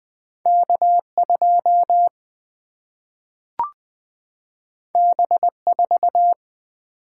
Morse code, heard as a keyed tone, about 20 wpm, 700 Hz. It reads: K 2 B 4
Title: Single-Letter Single-Number 20wpm